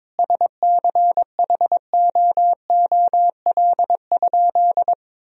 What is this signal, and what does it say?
Morse code, heard as a keyed tone, about 22 wpm, 700 Hz. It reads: SCHOOL?
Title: Sentences from Top 1000 Words - Review Part 1 - 22wpm